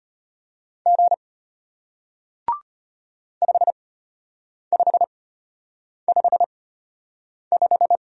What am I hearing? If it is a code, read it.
Morse code, heard as a keyed tone, about 34 wpm, spaced as for 8 wpm, 700 Hz. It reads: G 5555